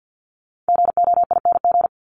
Morse code, heard as a keyed tone, about 50 wpm, 700 Hz. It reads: GOING
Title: ING Suffix - Top 100 Words with Base Words 50wpm